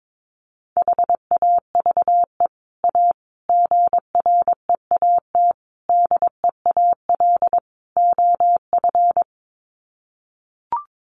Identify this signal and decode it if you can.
Morse code, heard as a keyed tone, about 22 wpm, 700 Hz. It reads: HAVE A GREAT DEAL OF
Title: Sets of 5 Words Encore2 22wpm